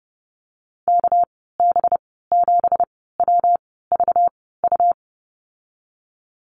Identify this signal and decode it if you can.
Morse code, heard as a keyed tone, about 30 wpm, spaced as for 17 wpm, 700 Hz. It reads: KB7WVU
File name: Lesson 37g - Letter x - Call Signs 30wpm